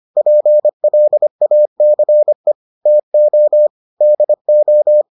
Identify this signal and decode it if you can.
Morse code, heard as a keyed tone, about 25 wpm, 600 Hz. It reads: PLACE TO DO